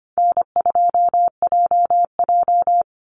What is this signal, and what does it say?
Morse code, heard as a keyed tone, about 25 wpm, 700 Hz. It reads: N2JJ
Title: Call Signs - 25wpm